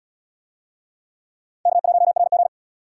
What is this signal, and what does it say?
Morse code, heard as a keyed tone, about 70 wpm, 700 Hz. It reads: K1RG